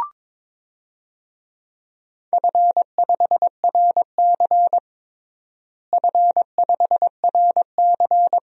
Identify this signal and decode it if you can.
Morse code, heard as a keyed tone, about 22 wpm, 700 Hz. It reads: F5RC F5RC
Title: DXCC-100 Call Signs - 4 Characters - Repeated 2x - 22wpm